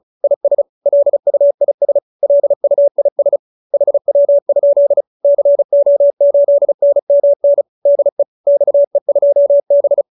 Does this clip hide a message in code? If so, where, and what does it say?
Yes: Morse code from 0.2 to 10.0 s, heard as a keyed tone, about 35 wpm, 590 Hz. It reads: IS LUIS LUIS HW? CO8NMN DE XE2B